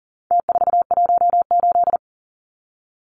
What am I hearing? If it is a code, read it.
Morse code, heard as a keyed tone, about 40 wpm, 700 Hz. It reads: T418
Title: Numbers - Contest Style - 4-Digit 40wpm